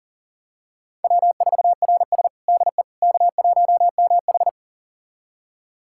Morse code, heard as a keyed tone, about 37 wpm, 700 Hz. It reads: W4RS DE K1MH
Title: USA Callsign DE Callsign - 4 Characters - 40wpm